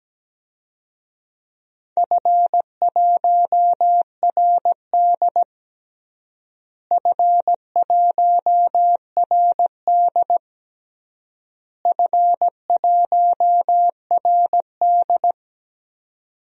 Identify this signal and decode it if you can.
Morse code, heard as a keyed tone, about 17 wpm, 700 Hz. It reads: F1RD F1RD F1RD
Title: DXCC-100 Call Signs - 4 Characters - Repeated 3x - 17wpm